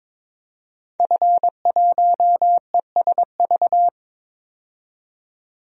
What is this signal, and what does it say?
Morse code, heard as a keyed tone, about 22 wpm, 700 Hz. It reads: F1ESV
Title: DXCC-100 Call Signs - 5 Characters - 22wpm